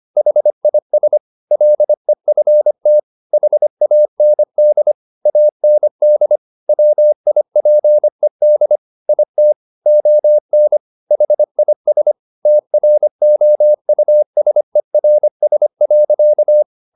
Morse code, heard as a keyed tone, about 25 wpm, 600 Hz. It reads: HIS LEFT HAND AND WIPED IT ON HIS TROUSERS.